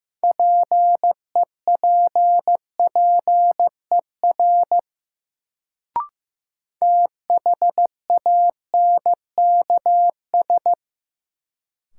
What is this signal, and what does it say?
Morse code, heard as a keyed tone, about 15 wpm, 700 Hz. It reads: PEPPER THANKS
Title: Sets of 5 Words - New Word Drill - 15wpm